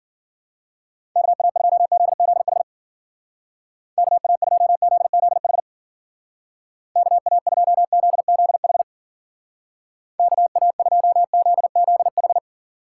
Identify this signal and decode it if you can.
Morse code, heard as a keyed tone, about 54 wpm, 700 Hz. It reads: KA2ZZH KA2ZZH KA2ZZH KA2ZZH